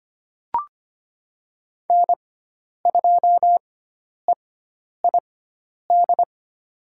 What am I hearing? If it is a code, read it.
Morse code, heard as a keyed tone, about 25 wpm, spaced as for 10 wpm, 700 Hz. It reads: N2EID
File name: Lesson 16g - Letter c - Call Signs 25wpm